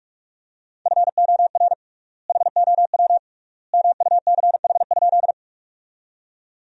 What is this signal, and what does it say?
Morse code, heard as a keyed tone, about 41 wpm, 700 Hz. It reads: FOR HOW MUCH?
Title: 3x Word-Spacing - Sentences from Top 200 Words - 45wpm